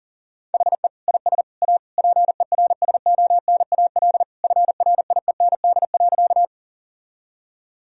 Morse code, heard as a keyed tone, about 40 wpm, 690 Hz. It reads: HE IS A PERSONAL FRIEND.
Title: Sentences from Top 1000 Words - Review Part 4 - 40wpm